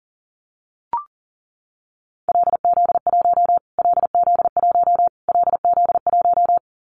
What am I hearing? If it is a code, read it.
Morse code, heard as a keyed tone, about 40 wpm, 700 Hz. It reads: LZ1 LZ1 LZ1
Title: Callsign Prefixes for Top 25 DXCC - Repeated 3x - 40wpm